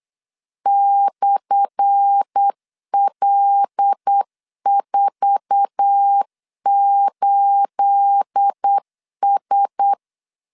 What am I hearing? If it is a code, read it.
Morse code, heard as a keyed tone, about 8 wpm, 800 Hz. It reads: /L48S
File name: LEZIONI / 10 SIGNS / 10A SLASH 8wpm 3452